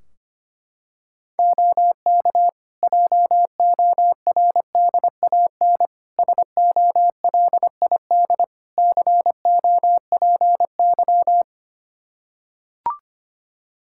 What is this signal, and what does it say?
Morse code, heard as a keyed tone, about 25 wpm, 700 Hz. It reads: OK JORDAN SOLID COPY
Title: Practice QSO - Exchange 3 - More Info - 25wpm